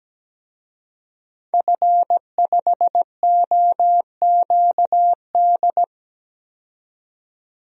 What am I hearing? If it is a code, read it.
Morse code, heard as a keyed tone, about 17 wpm, 700 Hz. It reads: F5OQD
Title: DXCC-100 Call Signs - 5 Characters - 17wpm